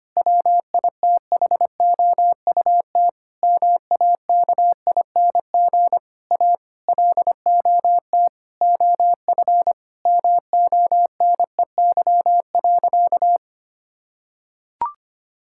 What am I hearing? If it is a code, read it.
Morse code, heard as a keyed tone, about 25 wpm, 700 Hz. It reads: WITHOUT MAKING A LOT OF MONEY.
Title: Sentences from Top 1000 Words - Review Part 6 - 25wpm